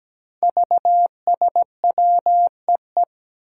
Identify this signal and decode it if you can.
Morse code, heard as a keyed tone, about 17 wpm, 700 Hz. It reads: VSWEE
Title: Five Challenging Letter-Number Combinations 17wpm